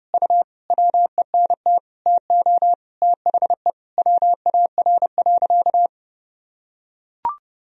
Morse code, heard as a keyed tone, about 32 wpm, 700 Hz. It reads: U WENT TO THE WAR.
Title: Sentences from Top 400 Words and Abbreviations 30wpm